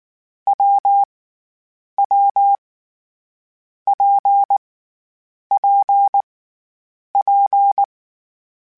Morse code, heard as a keyed tone, about 19 wpm, 800 Hz. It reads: W W P P P
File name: Variable Speed / Variable Speed-mp3s / WPJ WPJ1 sp @18